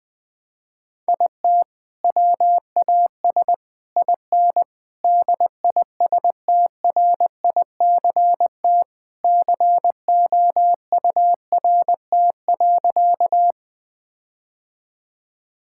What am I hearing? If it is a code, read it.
Morse code, heard as a keyed tone, about 20 wpm, 700 Hz. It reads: IT WAS IN DISTRICT COURT.
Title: Sentences from Top 1600 Words 20wpm